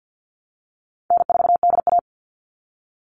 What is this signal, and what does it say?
Morse code, heard as a keyed tone, about 49 wpm, 700 Hz. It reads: N4DA
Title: NA SOTA Chaser Callsigns 50wpm